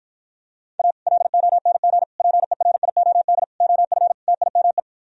Morse code, heard as a keyed tone, about 52 wpm, 700 Hz. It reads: A LONG PERIOD OF TIME